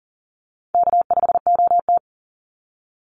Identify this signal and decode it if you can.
Morse code, heard as a keyed tone, about 40 wpm, 700 Hz. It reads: K5OT
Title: NA SOTA Chaser Callsigns 40wpm